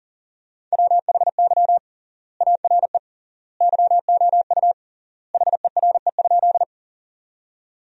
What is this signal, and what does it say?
Morse code, heard as a keyed tone, about 40 wpm, 700 Hz. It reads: WHY ARE YOU HERE?
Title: 3x Word-Spacing - Sentences from Top 200 Words - 40wpm